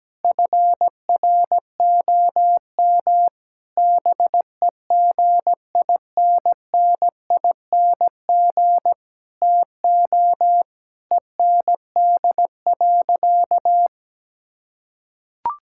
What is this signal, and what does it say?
Morse code, heard as a keyed tone, about 17 wpm, 700 Hz. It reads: FROM BEGINNING TO END.
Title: Sentences from Top 1000 Words - Review Part 5 - 17wpm